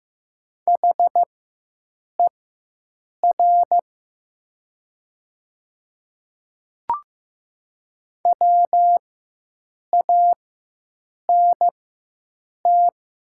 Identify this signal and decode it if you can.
Morse code, heard as a keyed tone, about 15 wpm, spaced as for 7 wpm, 700 Hz. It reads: HER WANT
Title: 4x Character Spacing - Top 200 Words - 15wpm